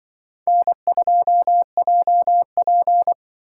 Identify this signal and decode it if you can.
Morse code, heard as a keyed tone, about 24 wpm, 700 Hz. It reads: N2JP
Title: Speed Racing - USA 1x2 Call Signs 20wpm